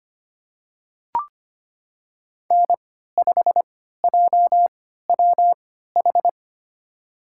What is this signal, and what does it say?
Morse code, heard as a keyed tone, about 25 wpm, spaced as for 14 wpm, 700 Hz. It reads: N5JWH
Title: USA Call Signs - 3x Character Spacing - 25wpm